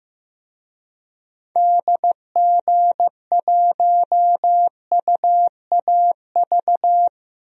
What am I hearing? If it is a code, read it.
Morse code, heard as a keyed tone, about 15 wpm, 700 Hz. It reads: DG1UAV
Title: DXCC-100 Call Signs - 6 Characters - Repeated 2x - 15wpm